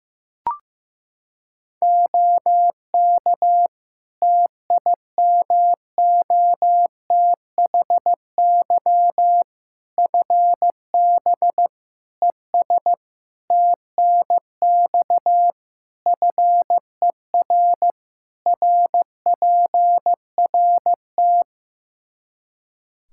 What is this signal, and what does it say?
Morse code, heard as a keyed tone, about 16 wpm, 700 Hz. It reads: OK TIMOTHY FB ES TNX FER RPRT
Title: K1USN SST QSO Practice 15wpm